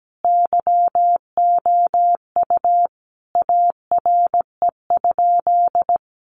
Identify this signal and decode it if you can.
Morse code, heard as a keyed tone, about 17 wpm, 700 Hz. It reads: YOU ARE?
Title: Sentences from Top 700 Words 17wpm